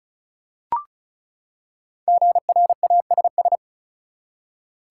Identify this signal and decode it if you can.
Morse code, heard as a keyed tone, about 36 wpm, 700 Hz. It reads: GRASS
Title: Five Letter Words - 35wpm